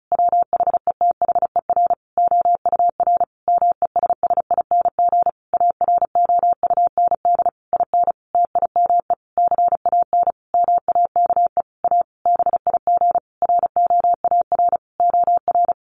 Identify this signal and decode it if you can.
Morse code, heard as a keyed tone, about 34 wpm, 700 Hz. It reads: WHETHER OUR MESSING AROUND IN TIME CAN MAKE A BIG ROAR OR
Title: A Sound Of Thunder 35wpm